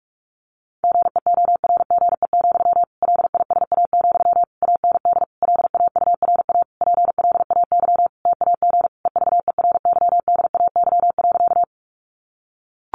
Morse code, heard as a keyed tone, about 44 wpm, 700 Hz. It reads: GEORGE, LISA, AND LAURA PLAY TAG EVERYDAY.
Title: Additional Easy Phrases 45wpm